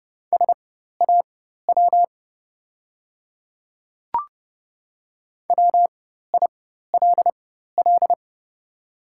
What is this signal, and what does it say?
Morse code, heard as a keyed tone, about 30 wpm, spaced as for 14 wpm, 700 Hz. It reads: SAW WILL
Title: 4x Character Spacing - Top 500 Words - 30wpm